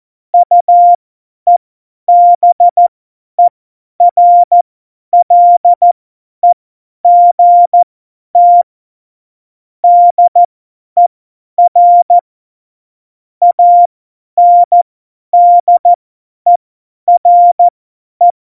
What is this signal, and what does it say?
Morse code, heard as a keyed tone, about 14 wpm, spaced as for 10 wpm, 700 Hz. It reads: UEBERLEGT DER ANDERE
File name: Morsekurs / Witze / Witz 31.Juli 2018, 50 F70